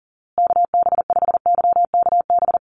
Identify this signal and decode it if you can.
Morse code, heard as a keyed tone, about 40 wpm, 700 Hz. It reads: KB5YKB
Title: USA Call Signs Repeated 2x 40wpm